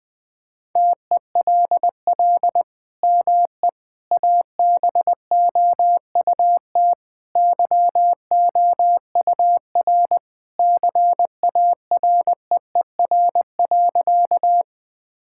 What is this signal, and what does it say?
Morse code, heard as a keyed tone, about 20 wpm, 700 Hz. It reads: TELL ME ABOUT YOUR CAREER.
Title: Sentences from Top 1000 Words - Review Part 6 - 20wpm